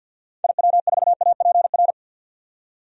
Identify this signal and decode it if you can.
Morse code, heard as a keyed tone, about 52 wpm, 700 Hz. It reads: IW4APR
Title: DXCC-100 Call Signs - 6 Characters - 50wpm